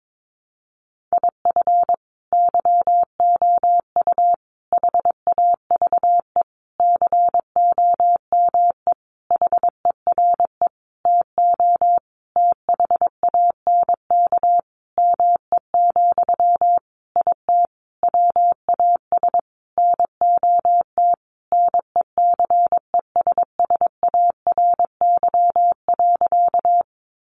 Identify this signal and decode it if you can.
Morse code, heard as a keyed tone, about 22 wpm, 700 Hz. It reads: IF YOU HAVE COME HERE TO THANK ME, IT WAS NOT NECESSARY.
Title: Sentences from Top 800 Words 22wpm